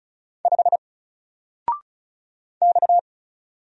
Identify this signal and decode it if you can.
Morse code, heard as a keyed tone, about 36 wpm, 700 Hz. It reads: 5 X
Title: Single Letter-Number 35wpm